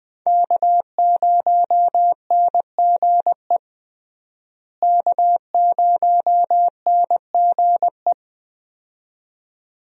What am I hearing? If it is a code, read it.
Morse code, heard as a keyed tone, about 20 wpm, 700 Hz. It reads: K0NGE K0NGE
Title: USA Call Signs - 5 Characters - Repeated 2x - 20wpm